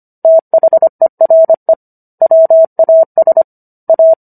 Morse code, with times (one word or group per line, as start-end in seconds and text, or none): 0.2-1.7 THERE
2.2-3.4 WAS
3.9-4.1 A